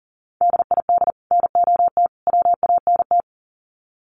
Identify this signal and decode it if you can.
Morse code, heard as a keyed tone, about 40 wpm, 700 Hz. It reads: DID NOT WANT
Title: Sets of 3 Words 40wpm